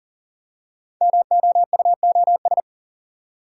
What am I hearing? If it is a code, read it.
Morse code, heard as a keyed tone, about 38 wpm, 700 Hz. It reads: MOUOS
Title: Code Groups - 5 Letters - 40wpm